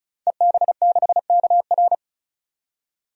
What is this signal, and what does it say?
Morse code, heard as a keyed tone, about 34 wpm, 700 Hz. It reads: EB6KR
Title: DXCC-100 Call Signs - 5 Characters - 35wpm